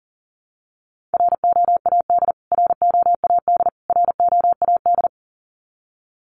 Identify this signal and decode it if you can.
Morse code, heard as a keyed tone, about 40 wpm, 700 Hz. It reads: ROAD ROAD ROAD
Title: Top 500 Words Repeated 40wpm